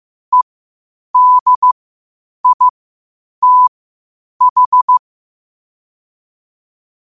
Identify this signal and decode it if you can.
Morse code, heard as a keyed tone, about 15 wpm, spaced as for 8 wpm, 1000 Hz. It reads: EDITH
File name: qotdfiles / 8wpm